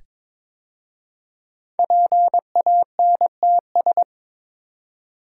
Morse code, heard as a keyed tone, about 22 wpm, 700 Hz. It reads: PANTS
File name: 5 Letter Words - First 3 Letters Given - 22wpm